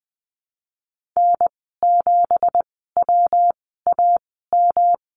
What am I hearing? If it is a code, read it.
Morse code, heard as a keyed tone, about 20 wpm, spaced as for 15 wpm, 700 Hz. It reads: N7WAM
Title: 173 Lesson 28g - Number 9 - Call Signs 20wpm